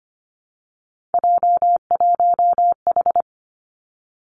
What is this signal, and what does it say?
Morse code, heard as a keyed tone, about 25 wpm, 700 Hz. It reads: J1H
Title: Three Challenging Letter-Number Combinations 25wpm